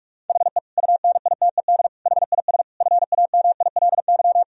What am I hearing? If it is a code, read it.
Morse code, heard as a keyed tone, about 46 wpm, 700 Hz. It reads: HE UNITED HIS FAMILY
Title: Sentences from Top 1000 Words - Review Part 4 - 45wpm